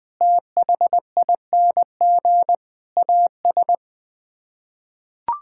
Morse code, heard as a keyed tone, about 21 wpm, 700 Hz. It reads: THING AS
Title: Sets of 6 Words - 20wpm